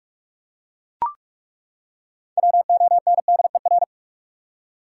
Morse code, heard as a keyed tone, about 42 wpm, 700 Hz. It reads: WONDER
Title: Six Letter Words - 45wpm